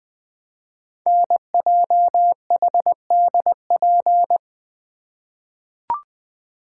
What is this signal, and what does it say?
Morse code, heard as a keyed tone, about 21 wpm, 700 Hz. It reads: NJHDP
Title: Code Groups - 5 Letters - 20wpm